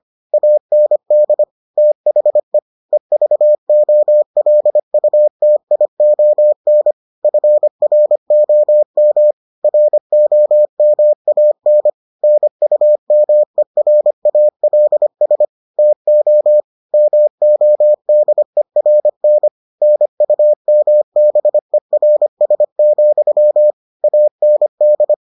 Morse code, heard as a keyed tone, about 25 wpm, 600 Hz. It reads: AND THE EVOLUTION FROM ROMAN NUMERALS TO MODERN NUMBERS, AND